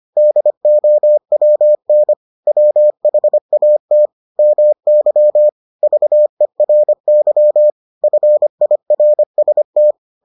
Morse code, heard as a keyed tone, about 25 wpm, 600 Hz. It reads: DOWN WHAT MY VERY FIRST